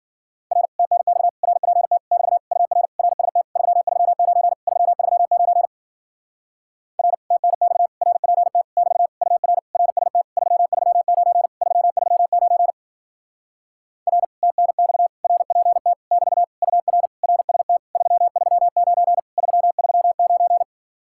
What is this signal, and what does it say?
Morse code, heard as a keyed tone, about 52 wpm, 700 Hz. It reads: R TNX RPT <BT> UR RST 339 339 R TNX RPT <BT> UR RST 339 339 R TNX RPT <BT> UR RST 339 339